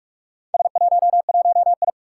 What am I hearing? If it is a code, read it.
Morse code, heard as a keyed tone, about 46 wpm, 700 Hz. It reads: S11I
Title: Four Challenging Letter-Number Combinations 45wpm